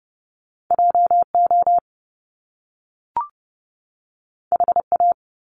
Morse code, heard as a keyed tone, about 30 wpm, 700 Hz. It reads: JO HA
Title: Top Two Letter Combinations in Words 30wpm